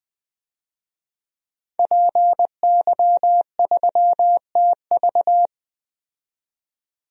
Morse code, heard as a keyed tone, about 20 wpm, 700 Hz. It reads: PY3TV
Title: DXCC-100 Call Signs - 5 Characters - 20wpm